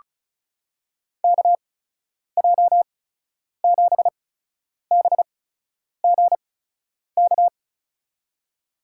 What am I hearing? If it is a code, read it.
Morse code, heard as a keyed tone, about 36 wpm, spaced as for 10 wpm, 700 Hz. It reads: KJ7BGK
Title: USA Call Signs - 6 Characters - 8x Character Spacing - 35wpm